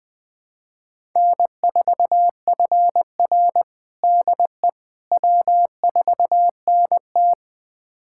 Morse code, heard as a keyed tone, about 21 wpm, 700 Hz. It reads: N4FR DE W4NT
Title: USA Callsign DE Callsign - 4 Characters - 20wpm